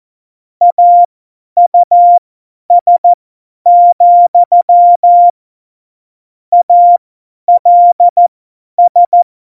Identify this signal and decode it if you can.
Morse code, heard as a keyed tone, about 14 wpm, spaced as for 10 wpm, 700 Hz. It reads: AUS, ALS
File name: Morsekurs / Witze / Witz 01. September 2020, 50 F70